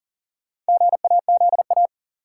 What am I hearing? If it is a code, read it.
Morse code, heard as a keyed tone, about 40 wpm, 700 Hz. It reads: GAZA
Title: Sets of 5 Words - New Word Drill - 40wpm